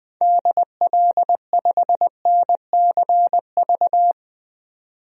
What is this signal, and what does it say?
Morse code, heard as a keyed tone, about 20 wpm, 700 Hz. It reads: DL5NCV
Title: DXCC-100 Call Signs - 6 Characters - 20wpm